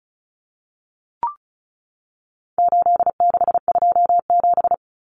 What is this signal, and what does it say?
Morse code, heard as a keyed tone, about 34 wpm, 700 Hz. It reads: 8627
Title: Numbers - 4-Digit 35wpm